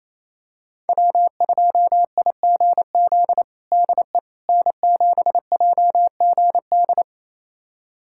Morse code, heard as a keyed tone, about 28 wpm, 700 Hz. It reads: W2IGZ DE N7JGD
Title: USA Callsign DE Callsign - 5 Characters - 28wpm